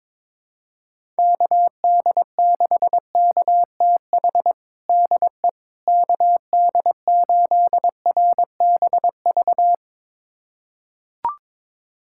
Morse code, heard as a keyed tone, about 22 wpm, 700 Hz. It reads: KD6KTH DE KD8RBV
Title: USA Callsign DE Callsign - 6 Characters - 22wpm